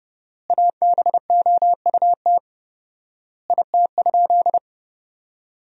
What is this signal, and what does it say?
Morse code, heard as a keyed tone, about 30 wpm, 700 Hz. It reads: ABOUT IT?
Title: 4x Word-Spacing - Sentences from Top 100 Words - 30wpm